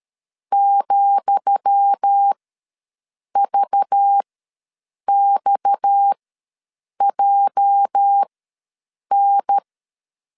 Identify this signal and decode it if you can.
Morse code, heard as a keyed tone, about 13 wpm, spaced as for 7 wpm, 800 Hz. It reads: ,VXJN
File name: LEZIONI / 10 SIGNS / 10D COMA ws 12wpm 3618